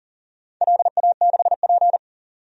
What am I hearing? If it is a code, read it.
Morse code, heard as a keyed tone, about 37 wpm, 700 Hz. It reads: LA6P
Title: DXCC-100 Call Signs - 4 Characters - 40wpm